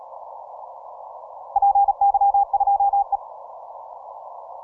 Morse code, heard as a keyed tone, about 35 wpm, 800 Hz. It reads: PY2E
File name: simple-file-list / 8Q7DX 15MCW